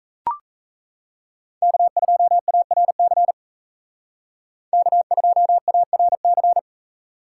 Morse code, heard as a keyed tone, about 34 wpm, 700 Hz. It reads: K2ARC K2ARC
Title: Speed Racing - USA Call Signs 28wpm